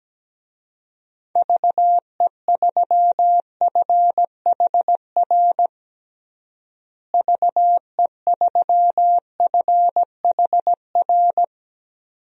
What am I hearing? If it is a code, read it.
Morse code, heard as a keyed tone, about 17 wpm, 700 Hz. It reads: VE3FHR VE3FHR
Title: DXCC-100 Call Signs - 6 Characters - Repeated 2x - 17wpm